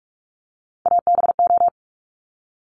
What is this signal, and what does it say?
Morse code, heard as a keyed tone, about 46 wpm, 690 Hz. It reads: ABO
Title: Top Three Letter Combinations in Words 45wpm